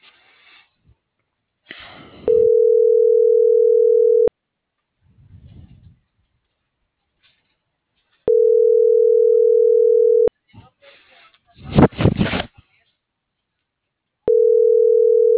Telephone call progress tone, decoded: ringback tone

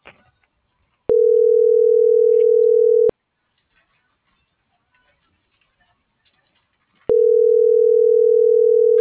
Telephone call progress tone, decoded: ringback tone

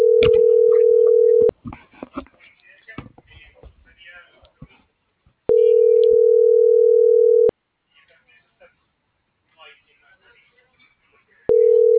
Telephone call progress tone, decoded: ringback tone

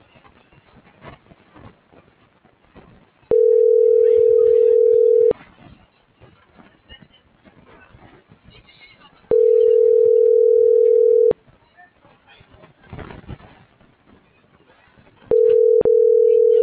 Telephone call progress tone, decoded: ringback tone

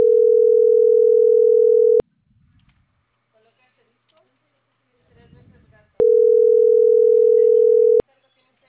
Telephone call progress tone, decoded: ringback tone

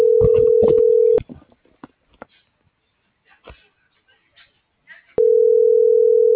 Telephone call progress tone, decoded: ringback tone

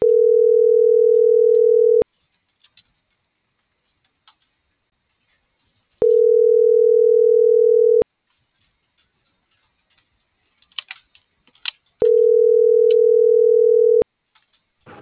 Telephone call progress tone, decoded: ringback tone

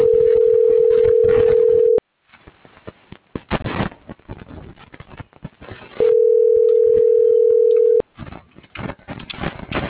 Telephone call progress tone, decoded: ringback tone

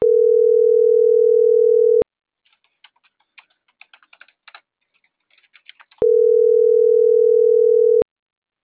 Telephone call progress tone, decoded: ringback tone

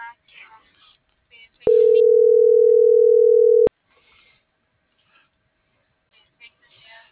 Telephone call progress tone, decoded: ringback tone

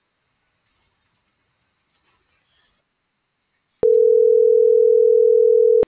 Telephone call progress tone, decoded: ringback tone

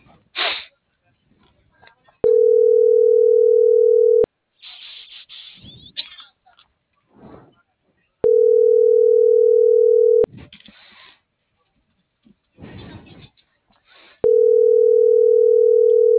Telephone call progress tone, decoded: ringback tone

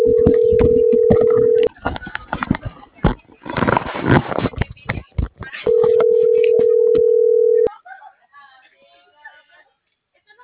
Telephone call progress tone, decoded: ringback tone

